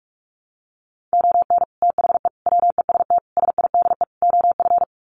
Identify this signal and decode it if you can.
Morse code, heard as a keyed tone, about 44 wpm, 700 Hz. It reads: ON THE WEST SIDE OF
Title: Sets of 5 Words Encore2 45wpm